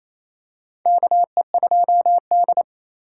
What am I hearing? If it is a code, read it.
Morse code, heard as a keyed tone, about 28 wpm, 700 Hz. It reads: KE2D
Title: K1USN SST QSO Practice - 28wpm